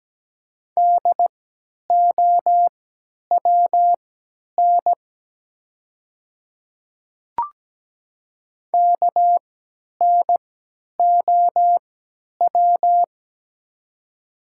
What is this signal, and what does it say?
Morse code, heard as a keyed tone, about 17 wpm, spaced as for 10 wpm, 700 Hz. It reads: DOWN KNOW
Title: 3x Character Spacing - Top 100 Words - 17wpm